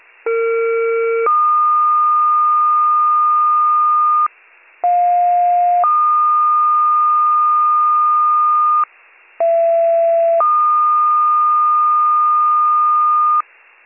Two-tone page, heard as a one-second tone, then a three-second tone, about 460 then 1220 Hz, about 710 then 1220 Hz, about 670 then 1220 Hz. It is heard from 0.3 to 13.4 s.